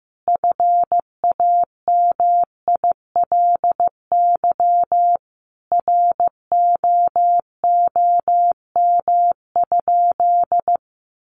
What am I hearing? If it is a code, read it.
Morse code, heard as a keyed tone, about 15 wpm, 700 Hz. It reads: FAMILY ROOM?
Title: Sentences from Top 1200 Words 15wpm